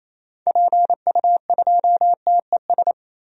Morse code, heard as a keyed tone, about 28 wpm, 700 Hz. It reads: PU2TES
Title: DXCC-100 Call Signs - 6 Characters - 28wpm